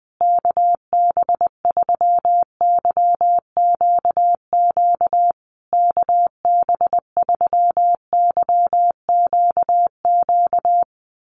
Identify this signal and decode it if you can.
Morse code, heard as a keyed tone, about 20 wpm, 700 Hz. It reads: KB3YQQ KB3YQQ
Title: USA Call Signs Repeated 3x 20wpm